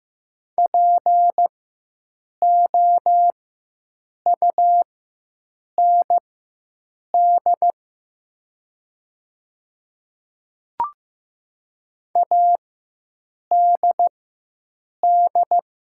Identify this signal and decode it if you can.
Morse code, heard as a keyed tone, about 15 wpm, spaced as for 7 wpm, 690 Hz. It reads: POUND ADD
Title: 4x Character Spacing - Top 500 Words - 15wpm